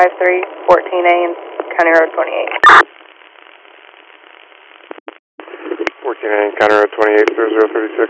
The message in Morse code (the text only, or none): none